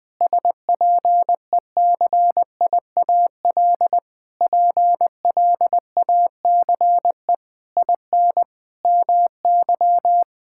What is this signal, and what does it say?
Morse code, heard as a keyed tone, about 20 wpm, 700 Hz. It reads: SPECIAL PLACE IN MY
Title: Sentences with 9-Word Phrases - Spoken Beginning - 20wpm